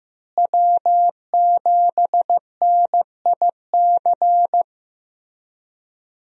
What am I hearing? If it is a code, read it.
Morse code, heard as a keyed tone, about 15 wpm, 700 Hz. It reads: W7NIC
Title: USA Call Signs - 5 Characters - Repeated 3x - 15wpm